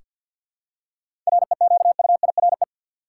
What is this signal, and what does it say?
Morse code, heard as a keyed tone, about 46 wpm, 700 Hz. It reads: REQUIRE
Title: Seven Letter Words - 50wpm